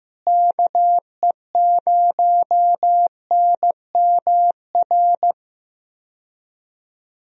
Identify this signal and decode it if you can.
Morse code, heard as a keyed tone, about 15 wpm, 700 Hz. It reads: KE0NMR
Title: USA Call Signs - 6 Characters - Repeated 2x - 15wpm